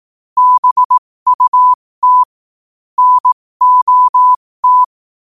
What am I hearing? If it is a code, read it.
Morse code, heard as a keyed tone, about 18 wpm, spaced as for 15 wpm, 1000 Hz. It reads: BUT NOT
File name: qotdfiles / 15wpm